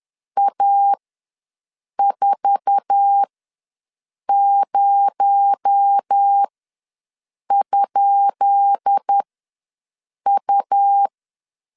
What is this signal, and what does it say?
Morse code, heard as a keyed tone, about 10 wpm, spaced as for 6 wpm, 800 Hz. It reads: A40?U
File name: LEZIONI / 10 SIGNS / 10C DOT ws 10wpm 4213